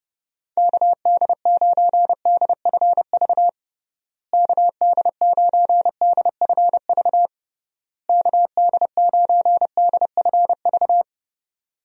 Morse code, heard as a keyed tone, about 29 wpm, 700 Hz. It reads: KD9DFV KD9DFV KD9DFV
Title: USA Call Signs - 6 Characters - Repeated 3x - 30wpm